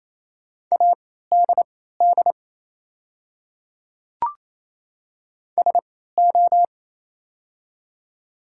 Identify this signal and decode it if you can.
Morse code, heard as a keyed tone, about 28 wpm, spaced as for 16 wpm, 700 Hz. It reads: ADD SO
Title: 3x Character Spacing - Top 200 Words - 28wpm